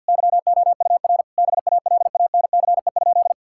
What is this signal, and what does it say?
Morse code, heard as a keyed tone, about 52 wpm, 700 Hz. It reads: YOUR BALANCE?
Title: Sentences from Top 1000 Words - Review Part 6 - 50wpm